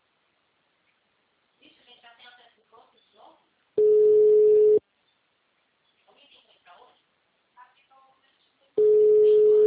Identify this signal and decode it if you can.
Ringback tone (Europe, 425 Hz), with speech.